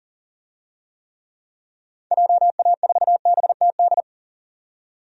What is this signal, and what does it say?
Morse code, heard as a keyed tone, about 40 wpm, 700 Hz. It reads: JA4BTD
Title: DXCC-100 Call Signs - 6 Characters - 40wpm